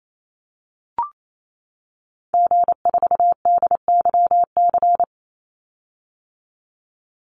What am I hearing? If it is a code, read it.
Morse code, heard as a keyed tone, about 28 wpm, 700 Hz. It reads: G4DYC
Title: Call Signs - 28wpm